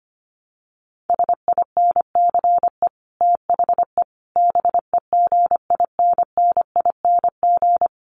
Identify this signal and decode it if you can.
Morse code, heard as a keyed tone, about 25 wpm, 700 Hz. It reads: SINCE THE BEGINNING
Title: Sets of 5 Words Encore2 25wpm